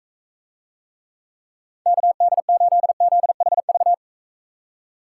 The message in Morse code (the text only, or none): KD8ZHV